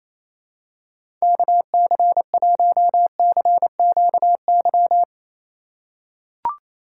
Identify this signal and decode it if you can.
Morse code, heard as a keyed tone, about 28 wpm, 700 Hz. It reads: KC1CQY
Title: USA Call Signs - 6 Characters - 4x Character Spacing - 28wpm